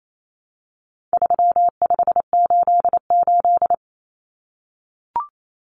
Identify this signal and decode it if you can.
Morse code, heard as a keyed tone, about 28 wpm, 700 Hz. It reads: 3588
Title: Common Cut Numbers - 4-Digit 28wpm